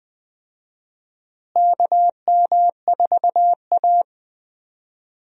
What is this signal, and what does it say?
Morse code, heard as a keyed tone, about 20 wpm, 700 Hz. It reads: KM4A
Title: USA Call Signs - 4 Characters - Repeated 2x - 20wpm